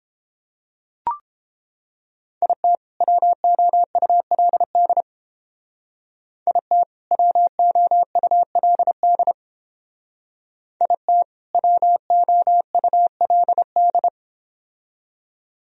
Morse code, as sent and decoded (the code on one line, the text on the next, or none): .. - / .-- --- ..- .-.. -.. / .. - / .-- --- ..- .-.. -.. / .. - / .-- --- ..- .-.. -..
IT WOULD IT WOULD IT WOULD